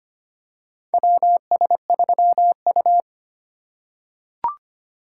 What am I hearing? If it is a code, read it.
Morse code, heard as a keyed tone, about 25 wpm, 700 Hz. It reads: WS3U
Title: USA Call Signs - 4 Characters - 2x Character Spacing - 25wpm